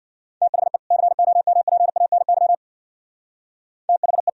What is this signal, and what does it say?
Morse code, heard as a keyed tone, about 59 wpm, 700 Hz. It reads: THE COMPANY THE
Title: Speed-Racing - Sets of 2 Words 50wpm